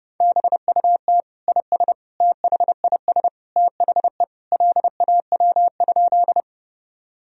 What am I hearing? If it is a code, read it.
Morse code, heard as a keyed tone, about 32 wpm, 700 Hz. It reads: BUT IS THIS THE LAW?
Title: Sentences from Top 1000 Words - Review Part 3 - 30wpm